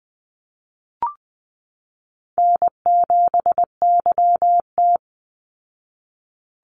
Morse code, heard as a keyed tone, about 20 wpm, 700 Hz. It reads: N7YT
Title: Call Signs Repeated 3x - 20wpm